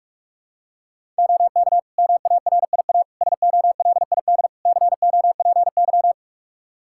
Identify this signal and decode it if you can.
Morse code, heard as a keyed tone, about 42 wpm, 700 Hz. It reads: OK MARIA SOLID COPY
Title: Practice QSO - Exchange 3 - More Info - 45wpm